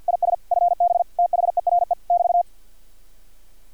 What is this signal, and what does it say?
Morse code, heard as a keyed tone, about 50 wpm, 700 Hz. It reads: IS FB THERE <BT>
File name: archived / CRSnet